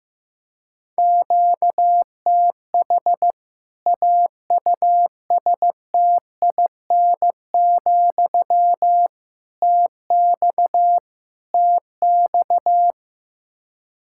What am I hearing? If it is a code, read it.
Morse code, heard as a keyed tone, about 15 wpm, 700 Hz. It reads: QTH AUSTIN, TX TX
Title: K1USN SST QSO Practice 15wpm